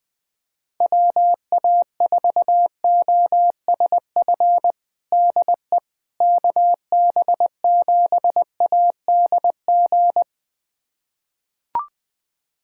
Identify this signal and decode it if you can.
Morse code, heard as a keyed tone, about 20 wpm, 700 Hz. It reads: WA4OSF DE KB7ADG
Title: USA Callsign DE Callsign - 6 Characters - 20wpm